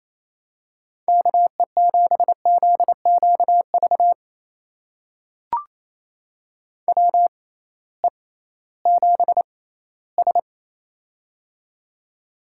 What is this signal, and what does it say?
Morse code, heard as a keyed tone, about 28 wpm, spaced as for 16 wpm, 700 Hz. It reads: KE7ZQV WE7S
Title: USA Call Signs - 6x Character Spacing - 28wpm